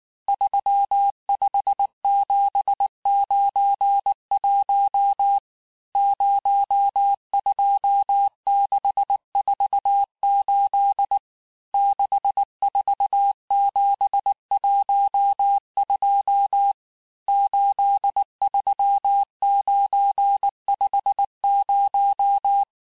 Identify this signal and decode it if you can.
Morse code, heard as a keyed tone, about 19 wpm, 800 Hz. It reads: 35791 02648 64712 83950
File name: cours-du-lundi files / cours-1589-texte-chiffre-vitesse-18mpm